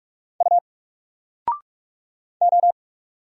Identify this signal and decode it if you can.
Morse code, heard as a keyed tone, about 44 wpm, 700 Hz. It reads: U O
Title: Single Letter-Number 45wpm